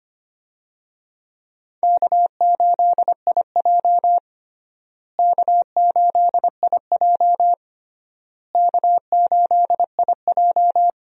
Morse code, heard as a keyed tone, about 25 wpm, 700 Hz. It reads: K8IJ K8IJ K8IJ
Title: USA Call Signs - 4 Characters - Repeated 3x - 25wpm